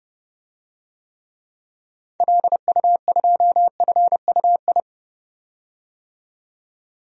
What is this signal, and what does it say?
Morse code, heard as a keyed tone, about 30 wpm, 700 Hz. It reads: LU2FUI
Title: DXCC-100 Call Signs - 6 Characters - 30wpm